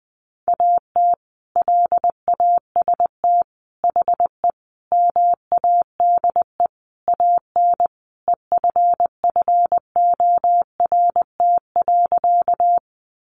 Morse code, heard as a keyed tone, about 20 wpm, 700 Hz. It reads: AT LAST HE MADE AN EFFORT.
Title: Sentences from Top 1200 Words 20wpm